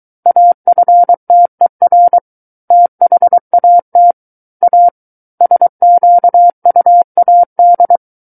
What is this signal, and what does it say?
Morse code, heard as a keyed tone, about 23 wpm, 700 Hz. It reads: AFTER THAT A SQUAD
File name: Book One / 14-Chapter 14 - 23 WPM